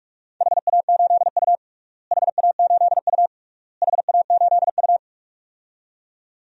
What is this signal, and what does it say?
Morse code, heard as a keyed tone, about 46 wpm, 700 Hz. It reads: HA8U HA8U HA8U
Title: DXCC-100 Call Signs - 4 Characters - Repeated 3x - 45wpm